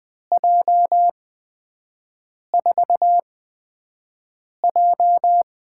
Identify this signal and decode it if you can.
Morse code, heard as a keyed tone, about 20 wpm, spaced as for 5 wpm, 700 Hz. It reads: J4J
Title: USA Call Signs - 8x Character Spacing - 20wpm